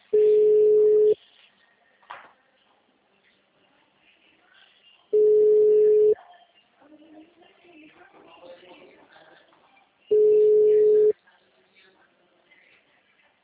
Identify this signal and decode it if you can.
Ringback tone (Europe, 425 Hz), with speech.